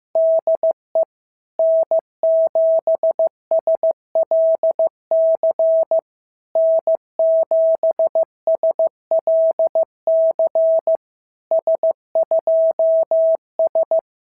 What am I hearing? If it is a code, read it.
Morse code, heard as a keyed tone, about 15 wpm, 660 Hz. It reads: DE N7SLC N7SLC S2S
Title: SOTA QSO Practice 15wpm